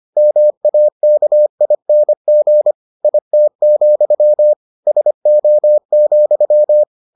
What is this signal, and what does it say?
Morse code, heard as a keyed tone, about 25 wpm, 600 Hz. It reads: MAKING IT, SO,